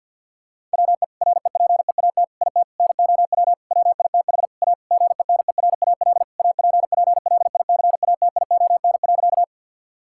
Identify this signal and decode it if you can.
Morse code, heard as a keyed tone, about 50 wpm, 700 Hz. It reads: WE REPEAT IT NOW WITH A GENERAL APPLICATION.
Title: Sentences from Top 1600 Words 50wpm